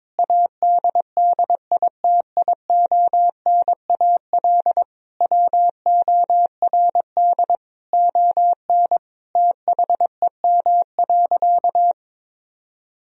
Morse code, heard as a keyed tone, about 22 wpm, 700 Hz. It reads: ADDITIONAL WORD ON THEM.
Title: Sentences from Top 1000 Words - Review Part 4 - 22wpm